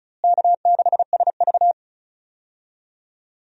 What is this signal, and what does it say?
Morse code, heard as a keyed tone, about 34 wpm, 700 Hz. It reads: K6SV
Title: USA Call Signs - 4 Characters - Repeated 2x - 35wpm